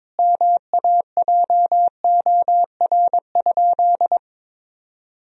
Morse code, heard as a keyed tone, about 22 wpm, 700 Hz. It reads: MAJOR?
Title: Sentences from Top 1000 Words - Review Part 4 - 22wpm